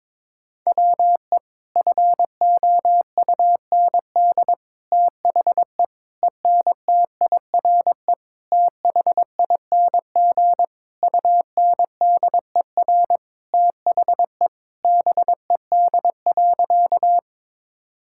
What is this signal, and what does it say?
Morse code, heard as a keyed tone, about 22 wpm, 700 Hz. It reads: WE FOUND THE ENTIRE THING UNDER THE BED.
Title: Sentences from Top 1000 Words - Review Part 5 - 22wpm